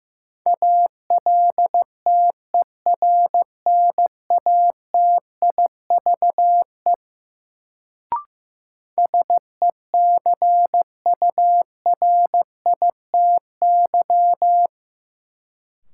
Morse code, heard as a keyed tone, about 15 wpm, 700 Hz. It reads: ALTERNATIVE SECURITY
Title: Sentences from Top 1000 Words - Review Part 4 - New Word Drill - 15wpm